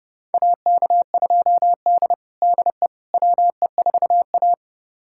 Morse code, heard as a keyed tone, about 32 wpm, 700 Hz. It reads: AK2D DE WE4A
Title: USA Callsign DE Callsign - 4 Characters - 30wpm